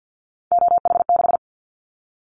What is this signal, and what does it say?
Morse code, heard as a keyed tone, about 50 wpm, 700 Hz. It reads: OH6
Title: Callsign Prefixes for Top 25 DXCC - 50wpm